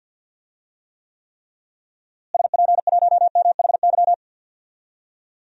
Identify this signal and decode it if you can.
Morse code, heard as a keyed tone, about 49 wpm, 700 Hz. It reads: SP1MHY
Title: DXCC-100 Call Signs - 6 Characters - Repeated 3x - 50wpm